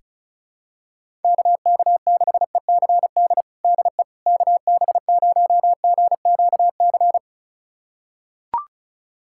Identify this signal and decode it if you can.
Morse code, heard as a keyed tone, about 36 wpm, 700 Hz. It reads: KK6ECD DE KB0GQC
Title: USA Callsign DE Callsign - 6 Characters - 35wpm